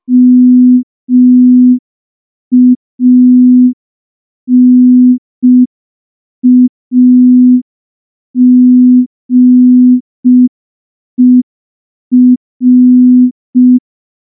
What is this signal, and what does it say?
Morse code, heard as a keyed tone, about 5 wpm, 250 Hz. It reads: MANAGER
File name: morse / morse-training-05WPM